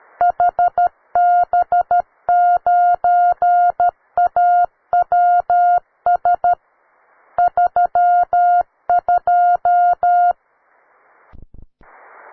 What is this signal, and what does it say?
Morse code, heard as a keyed tone, about 13 wpm, 710 Hz. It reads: HB9AWS 32
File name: wae / cw beginner / ua9cdv cw